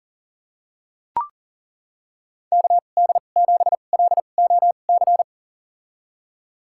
Morse code, heard as a keyed tone, about 40 wpm, spaced as for 29 wpm, 700 Hz. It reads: KD7LOC
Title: USA Call Signs - 6 Characters - 2x Character Spacing - 40wpm